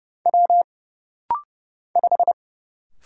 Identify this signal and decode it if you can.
Morse code, heard as a keyed tone, about 30 wpm, 700 Hz. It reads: W 5
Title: Single Letter-Number Alt - Rapid-Fire 30wpm